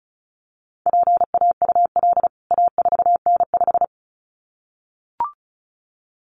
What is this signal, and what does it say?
Morse code, heard as a keyed tone, about 34 wpm, 700 Hz. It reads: PAUL A4N5
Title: CWT Exchange 35wpm